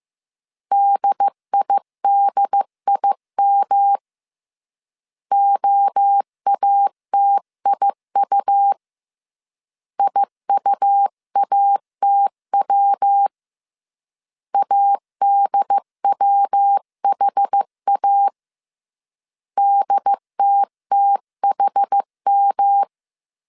Morse code, heard as a keyed tone, about 15 wpm, 800 Hz. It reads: DIDIM OATIU IUATW ADWHA DTTHM
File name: CW / DET / 1 3 1500C S A 14wpm 2447